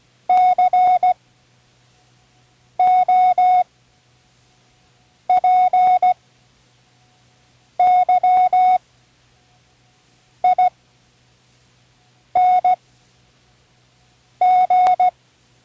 Morse code, heard as a keyed tone, about 16 wpm, spaced as for 5 wpm, 730 Hz. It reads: COPYING
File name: Learning Morse Code (from Tune in the World)